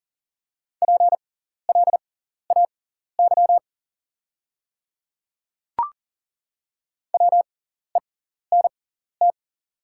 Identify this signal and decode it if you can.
Morse code, heard as a keyed tone, about 40 wpm, spaced as for 14 wpm, 700 Hz. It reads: PLAY WENT
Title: 6x Character Spacing - Top 300 Words - 40wpm